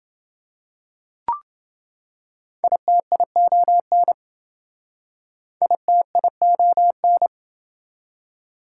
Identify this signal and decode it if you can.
Morse code, heard as a keyed tone, about 27 wpm, 700 Hz. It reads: ITION ITION